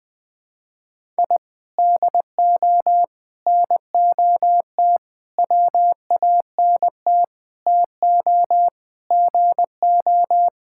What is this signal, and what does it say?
Morse code, heard as a keyed tone, about 21 wpm, 700 Hz. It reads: I DO NOT WANT TO GO
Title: Sentences with 4-Word Phrases - 20wpm